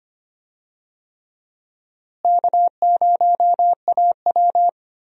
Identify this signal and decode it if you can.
Morse code, heard as a keyed tone, about 25 wpm, 700 Hz. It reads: K0AW
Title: USA Call Signs - 4 Characters - Repeated 3x - 25wpm